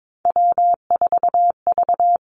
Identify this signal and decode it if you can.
Morse code, heard as a keyed tone, about 22 wpm, 700 Hz. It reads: W4V
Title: Three Challenging Letter-Number Combinations 22wpm